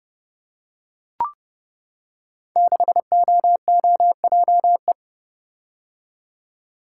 Morse code, heard as a keyed tone, about 30 wpm, 700 Hz. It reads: 6OOJE
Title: Code Groups - 5 Letter:Numbers - 30wpm